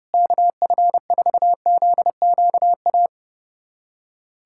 Morse code, heard as a keyed tone, about 30 wpm, 700 Hz. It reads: KF4ZQA
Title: USA Callsign DE Callsign 30wpm